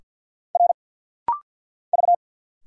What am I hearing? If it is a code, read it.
Morse code, heard as a keyed tone, about 50 wpm, 690 Hz. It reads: R V
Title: Single Letters - Rapid-Fire 50wpm